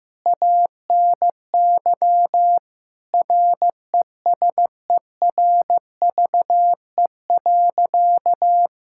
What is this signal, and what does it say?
Morse code, heard as a keyed tone, about 15 wpm, 700 Hz. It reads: ANY RESERVE.
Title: Sentences from Top 1000 Words - Review Part 6 - 15wpm